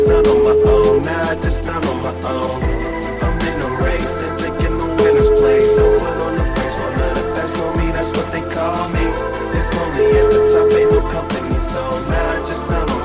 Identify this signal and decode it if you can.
Ringback tone (Europe, 425 Hz), with noise.